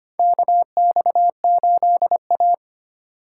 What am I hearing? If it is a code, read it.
Morse code, heard as a keyed tone, about 25 wpm, 700 Hz. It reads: KX8A
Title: USA Call Signs - 4 Characters - 8x Character Spacing - 25wpm